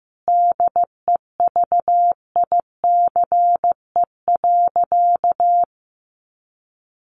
Morse code, heard as a keyed tone, about 14 wpm, 700 Hz. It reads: DEVICE.
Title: Sentences from Top 1500 Words 15wpm